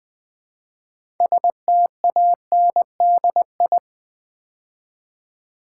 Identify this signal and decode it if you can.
Morse code, heard as a keyed tone, about 20 wpm, 700 Hz. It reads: STANDI
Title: Build A Word - 2 Letters At A Time - 8-Letter Words - 20wpm